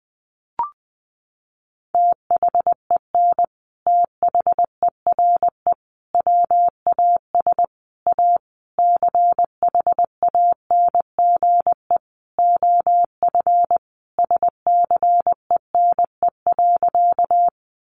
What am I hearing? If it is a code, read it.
Morse code, heard as a keyed tone, about 20 wpm, 700 Hz. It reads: THEN THERE WAS A CHANGE OF SCENE.
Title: Sentences from Top 1100 Words 20wpm